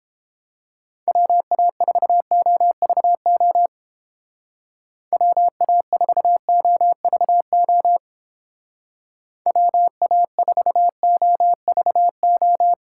Morse code, heard as a keyed tone, about 29 wpm, 700 Hz. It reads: WA4OVO WA4OVO WA4OVO